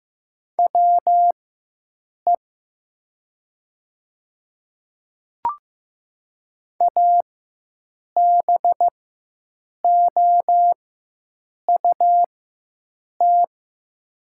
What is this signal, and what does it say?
Morse code, heard as a keyed tone, about 15 wpm, spaced as for 7 wpm, 700 Hz. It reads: WE ABOUT
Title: 4x Character Spacing - Top 100 Words - 15wpm